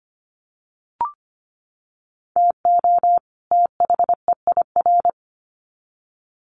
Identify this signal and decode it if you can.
Morse code, heard as a keyed tone, about 25 wpm, 700 Hz. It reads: TO THEIR
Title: Sets of 2 Words 25wpm